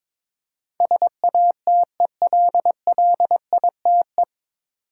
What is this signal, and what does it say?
Morse code, heard as a keyed tone, about 22 wpm, 700 Hz. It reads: SATELLITE
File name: 9 Letter Words - First 3 Letters Given - 22wpm